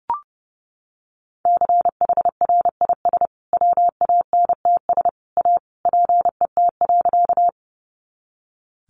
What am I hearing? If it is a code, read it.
Morse code, heard as a keyed tone, about 30 wpm, 700 Hz. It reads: CHRIS WANTS A PET.
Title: Easy Phrases 30wpm